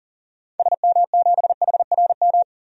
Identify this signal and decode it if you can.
Morse code, heard as a keyed tone, about 37 wpm, 700 Hz. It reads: SM7HRM
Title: DXCC-100 Call Signs - 6 Characters - 40wpm